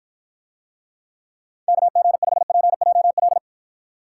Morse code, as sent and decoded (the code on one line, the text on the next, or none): -.- --. ..... .--. .--- .-..
KG5PJL